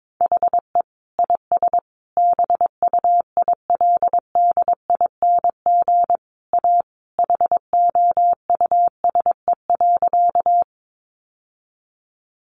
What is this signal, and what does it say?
Morse code, heard as a keyed tone, about 22 wpm, 700 Hz. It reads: HE IS BUILDING A HOUSE.
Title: Sentences from Top 700 Words 22wpm